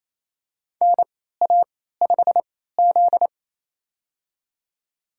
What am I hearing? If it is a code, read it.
Morse code, heard as a keyed tone, about 27 wpm, spaced as for 16 wpm, 690 Hz. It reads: NA5Z
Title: USA Call Signs - 3x Character Spacing - 28wpm